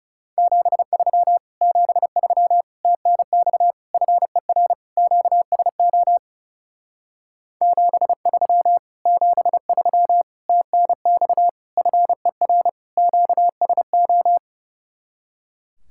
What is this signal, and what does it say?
Morse code, heard as a keyed tone, about 36 wpm, 700 Hz. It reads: 73 73 TNX FER QSO 73 73 TNX FER QSO
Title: Speed-Racing Common QSO Phrases - 30wpm